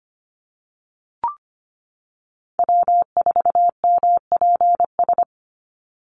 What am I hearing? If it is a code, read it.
Morse code, heard as a keyed tone, about 25 wpm, 700 Hz. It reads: W4MPS
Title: NA SOTA Chaser Callsigns 25wpm